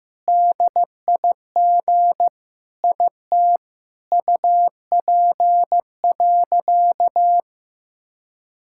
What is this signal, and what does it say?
Morse code, heard as a keyed tone, about 15 wpm, 700 Hz. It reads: DIG IT UP.
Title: Sentences from Top 1600 Words 15wpm